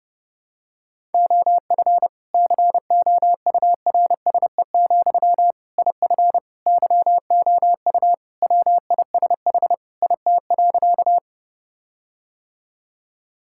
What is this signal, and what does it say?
Morse code, heard as a keyed tone, about 30 wpm, 700 Hz. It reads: OF COURSE, IF YOU WISH IT.
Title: Sentences from Top 1600 Words 30wpm